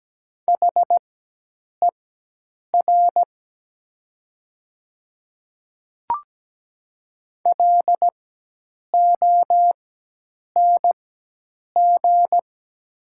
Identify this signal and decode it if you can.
Morse code, heard as a keyed tone, about 17 wpm, spaced as for 8 wpm, 700 Hz. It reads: HER LONG
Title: 4x Character Spacing - Top 200 Words - 17wpm